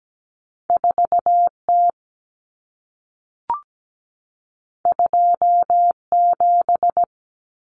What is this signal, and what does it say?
Morse code, heard as a keyed tone, about 17 wpm, 700 Hz. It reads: 4T 27
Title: Common Cut Numbers - 2-Digit 17wpm